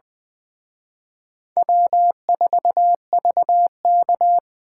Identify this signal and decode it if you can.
Morse code, heard as a keyed tone, about 24 wpm, 700 Hz. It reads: W4VK
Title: Speed Racing - USA 1x2 Call Signs 20wpm